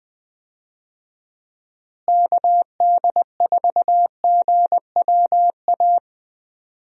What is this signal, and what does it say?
Morse code, heard as a keyed tone, about 20 wpm, 700 Hz. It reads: KD4GWA
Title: USA Call Signs - 6 Characters - Repeated 2x - 20wpm